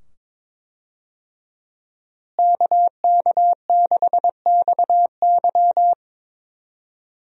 Morse code, heard as a keyed tone, about 22 wpm, 700 Hz. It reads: KK6XY
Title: USA Call Signs - 5 Characters - Repeated 2x - 22wpm